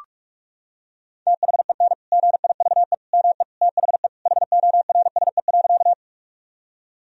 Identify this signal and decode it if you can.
Morse code, heard as a keyed tone, about 41 wpm, 700 Hz. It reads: THEN GIVE ME THE HORSE.
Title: Sentences from Top 1000 Words - Review Part 1 - 45wpm